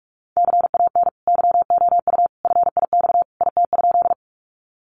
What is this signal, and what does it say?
Morse code, heard as a keyed tone, about 44 wpm, 700 Hz. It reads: CAN YOU FIX IT?
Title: Sentences from Top 1400 Words 45wpm